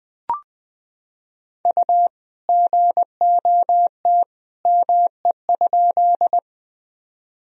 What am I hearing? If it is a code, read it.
Morse code, heard as a keyed tone, about 20 wpm, 700 Hz. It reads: U GOT ME?
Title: Sentences from Top 300 Words and Abbreviations 20wpm